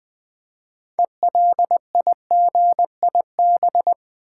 Morse code, heard as a keyed tone, about 20 wpm, 700 Hz. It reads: ELIGIB
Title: Build A Word - 2 Letters At A Time - 8-Letter Words - 20wpm